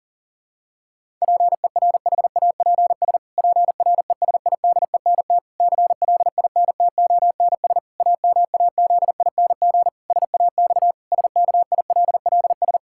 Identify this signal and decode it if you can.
Morse code, heard as a keyed tone, about 40 wpm, 700 Hz. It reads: PERHAPS PRESIDENT CLINTONS AMAZING SAX SKILLS
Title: Pangram Challenge v2 - 40wpm